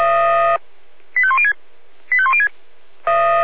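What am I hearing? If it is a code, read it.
Five-tone selective call (ZVEI), heard as quick runs of five tones, tones of about 70 ms. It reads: 85106 85106
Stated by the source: ZVEI five-tone selective call received on an SDR.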